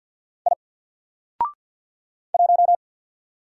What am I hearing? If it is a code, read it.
Morse code, heard as a keyed tone, about 48 wpm, 700 Hz. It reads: I 1
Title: Single Letter-Number Alt 50wpm